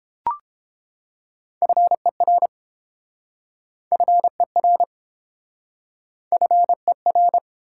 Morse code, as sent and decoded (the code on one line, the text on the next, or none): ..-. .-. / ..-. .-. / ..-. .-.
FER FER FER